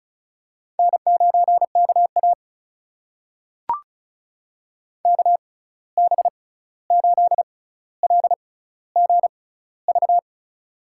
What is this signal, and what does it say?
Morse code, heard as a keyed tone, about 36 wpm, spaced as for 16 wpm, 700 Hz. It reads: N9KA KB8LGV
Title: USA Call Signs - 6x Character Spacing - 35wpm